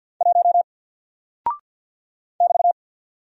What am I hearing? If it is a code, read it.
Morse code, heard as a keyed tone, about 49 wpm, 700 Hz. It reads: 1 <BT>
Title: Single Character (Letter:Number:Punctuation:Prosign) Alt 50wpm